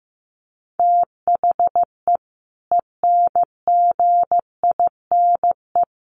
Morse code, heard as a keyed tone, about 15 wpm, 700 Hz. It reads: THE ENGINE
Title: Sentences from Top 1500 Words 15wpm